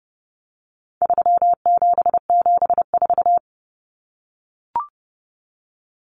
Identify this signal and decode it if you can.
Morse code, heard as a keyed tone, about 30 wpm, 700 Hz. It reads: 3774
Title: Common Cut Numbers - 4-Digit 30wpm